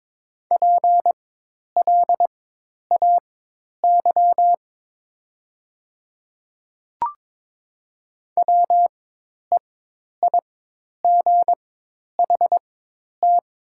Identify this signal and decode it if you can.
Morse code, heard as a keyed tone, about 22 wpm, spaced as for 10 wpm, 700 Hz. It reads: PLAY WEIGHT
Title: 4x Character Spacing - Top 500 Words - 22wpm